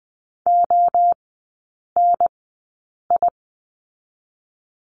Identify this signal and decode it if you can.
Morse code, heard as a keyed tone, about 20 wpm, 700 Hz. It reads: O N I
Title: 029 Lesson 6e - Letter i - Three Characters 20wpm